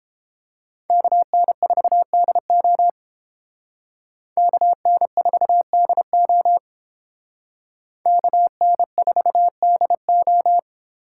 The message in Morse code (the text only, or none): KN4DO KN4DO KN4DO